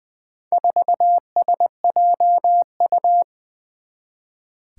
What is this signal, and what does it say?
Morse code, heard as a keyed tone, about 20 wpm, 700 Hz. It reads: 4SJU
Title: Four Challenging Letter-Number Combinations 20wpm